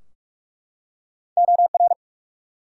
Morse code, heard as a keyed tone, about 44 wpm, 700 Hz. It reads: OR
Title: Two Letter Words - 45wpm